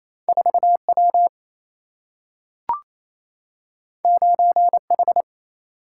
Morse code, heard as a keyed tone, about 28 wpm, 700 Hz. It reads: 4 W 9 H
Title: Single-Letter Single-Number 28wpm